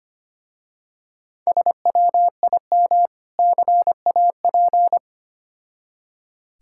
Morse code, heard as a keyed tone, about 25 wpm, 700 Hz. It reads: SWIM CAP
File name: practice / 25wpm / Difficult Words with Hints - 25wpm